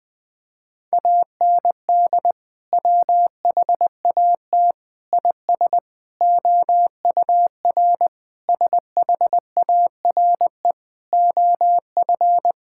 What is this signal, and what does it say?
Morse code, heard as a keyed tone, about 20 wpm, 700 Hz. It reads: AND WHAT IS OUR SHARE OF
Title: Sentences from Top 1000 Words - Review Part 6 - 20wpm